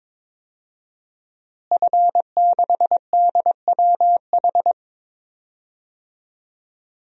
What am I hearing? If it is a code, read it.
Morse code, heard as a keyed tone, about 22 wpm, 700 Hz. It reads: F6DWH
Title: DXCC-100 Call Signs - 5 Characters - Repeated 2x - 22wpm